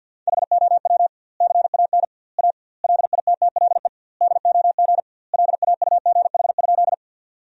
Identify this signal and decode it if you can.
Morse code, heard as a keyed tone, about 50 wpm, 700 Hz. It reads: HOW CAN A LITTLE DOG LAUGH?
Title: 2x Word-Spacing - Sentences from Top 500 Words - 50wpm